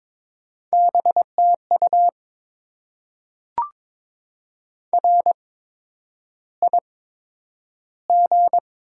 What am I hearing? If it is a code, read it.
Morse code, heard as a keyed tone, about 22 wpm, spaced as for 9 wpm, 700 Hz. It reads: BTU RIG
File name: Top 20 QSO Elements - 8x Character Spacing - 22wpm